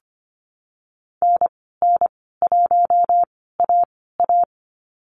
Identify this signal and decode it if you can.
Morse code, heard as a keyed tone, about 25 wpm, spaced as for 16 wpm, 700 Hz. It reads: NN1AA
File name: Lesson 15g - Number 5 - Call Signs 25wpm